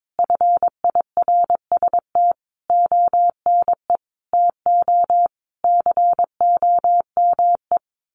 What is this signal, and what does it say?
Morse code, heard as a keyed tone, about 22 wpm, 700 Hz. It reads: FIRST ONE TO COME
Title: Sentences from Top 1300 Words 22wpm